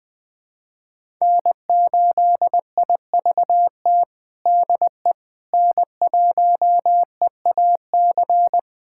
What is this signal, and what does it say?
Morse code, heard as a keyed tone, about 21 wpm, 700 Hz. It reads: N8IVT DE N1EAC
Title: USA Callsign DE Callsign - 5 Characters - 20wpm